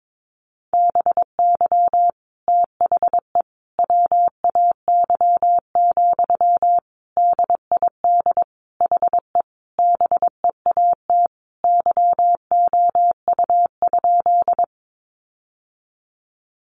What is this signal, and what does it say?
Morse code, heard as a keyed tone, about 22 wpm, 700 Hz. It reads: BY THE WAY, DID HE BEAT YOU?
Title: Sentences from Top 1300 Words 22wpm